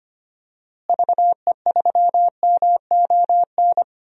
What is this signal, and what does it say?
Morse code, heard as a keyed tone, about 25 wpm, 700 Hz. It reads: VE3MON
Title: DXCC-100 Call Signs - 6 Characters - Repeated 3x - 25wpm